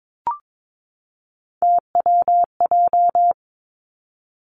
Morse code, heard as a keyed tone, about 22 wpm, 700 Hz. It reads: TWJ
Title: Three Challenging Letter-Number Combinations 22wpm